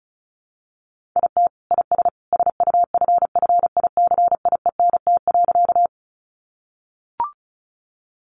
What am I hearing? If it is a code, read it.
Morse code, heard as a keyed tone, about 34 wpm, 700 Hz. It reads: IT IS SUFFICIENT.
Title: Sentences from Top 1400 Words 35wpm